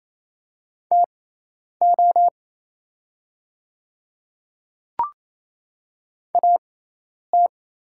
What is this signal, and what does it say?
Morse code, heard as a keyed tone, about 27 wpm, spaced as for 10 wpm, 690 Hz. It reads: TO AT
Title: 6x Character Spacing - Top 100 Words - 28wpm